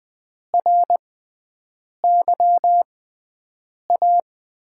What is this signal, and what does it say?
Morse code, heard as a keyed tone, about 20 wpm, spaced as for 7 wpm, 700 Hz. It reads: RYA
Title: USA Call Signs - 5 Characters - 6x Character Spacing - 20wpm